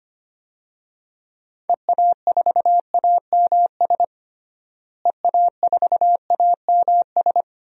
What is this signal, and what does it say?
Morse code, heard as a keyed tone, about 25 wpm, 700 Hz. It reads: EA4AMS EA4AMS
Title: DXCC-100 Call Signs - 6 Characters - Repeated 3x - 25wpm